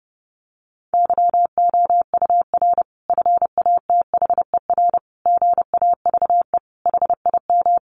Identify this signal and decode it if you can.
Morse code, heard as a keyed tone, about 30 wpm, 700 Hz. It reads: YOUR FATHER GAVE HIM
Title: Sentences from Top 1400 Words 30wpm